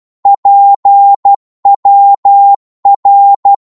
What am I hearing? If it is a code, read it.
Morse code, heard as a keyed tone, about 12 wpm, 800 Hz. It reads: PWR
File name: Morse / qso-12wpm